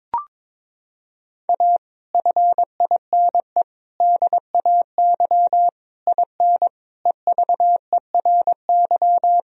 Morse code, heard as a keyed tone, about 21 wpm, 700 Hz. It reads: A FINE DAY IN EVERY
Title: Sentences from Top 1000 Words - Review Part 4 - 22wpm